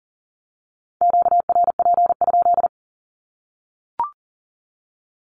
Morse code, heard as a keyed tone, about 40 wpm, 700 Hz. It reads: QRP?
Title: Q-Codes 40wpm